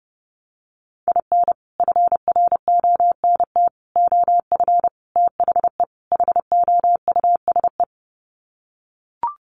Morse code, heard as a keyed tone, about 30 wpm, 700 Hz. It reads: IN FRONT OF THE HOUSE
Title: Sets of 5 Words Encore2 30wpm